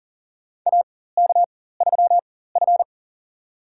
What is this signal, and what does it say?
Morse code, heard as a keyed tone, about 40 wpm, spaced as for 19 wpm, 700 Hz. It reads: AK3F
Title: USA Call Signs - 4 Characters - 4x Character Spacing - 40wpm